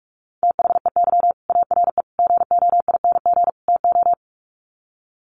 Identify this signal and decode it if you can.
Morse code, heard as a keyed tone, about 44 wpm, 700 Hz. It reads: THEY ARE GOING TO
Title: Sets of 4 Words 45wpm